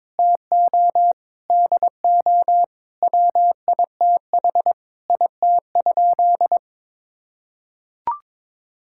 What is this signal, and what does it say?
Morse code, heard as a keyed tone, about 22 wpm, 700 Hz. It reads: TO DO WITH IT?
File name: practice / 22wpm / Sentences with 4-Word Phrases - 22wpm